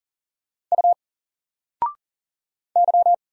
Morse code, heard as a keyed tone, about 38 wpm, 700 Hz. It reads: U Y
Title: Single Character (Letter:Number:Punctuation:Prosign) 40wpm